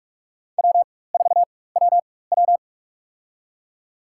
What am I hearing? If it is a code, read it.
Morse code, heard as a keyed tone, about 46 wpm, spaced as for 21 wpm, 700 Hz. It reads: W4WW
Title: USA Call Signs - 4 Characters - 4x Character Spacing - 45wpm